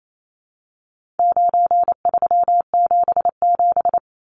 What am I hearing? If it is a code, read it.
Morse code, heard as a keyed tone, about 28 wpm, 700 Hz. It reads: 9377
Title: Numbers - 4-Digit 28wpm